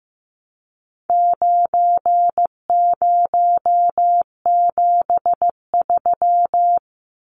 Morse code, heard as a keyed tone, about 15 wpm, 700 Hz. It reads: 9073
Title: Numbers - 4-Digit 15wpm